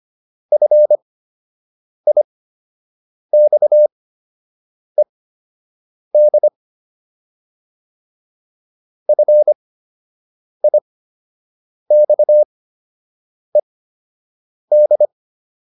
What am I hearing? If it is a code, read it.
Morse code, heard as a keyed tone, about 25 wpm, spaced as for 7 wpm, 600 Hz. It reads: FIXED FIXED